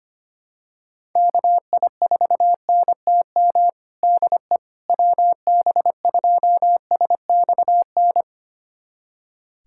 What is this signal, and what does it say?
Morse code, heard as a keyed tone, about 25 wpm, 700 Hz. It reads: KI4NTM DE WB2SXN
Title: USA Callsign DE Callsign 25wpm